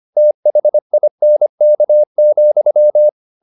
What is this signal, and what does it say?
Morse code, heard as a keyed tone, about 25 wpm, 600 Hz. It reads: THINK,